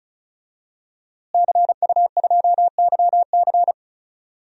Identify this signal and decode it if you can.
Morse code, heard as a keyed tone, about 36 wpm, 700 Hz. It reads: CU2YC
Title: DXCC-100 Call Signs - 5 Characters - Repeated 2x - 35wpm